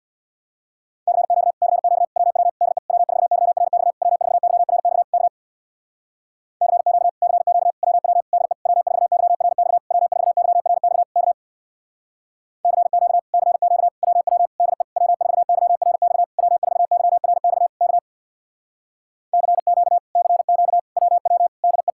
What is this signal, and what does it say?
Morse code, heard as a keyed tone, about 64 wpm, 700 Hz. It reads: CQ CQ WW DE W4YRX W4YRX K CQ CQ WW DE W4YRX W4YRX K CQ CQ WW DE W4YRX W4YRX K CQ CQ WW DE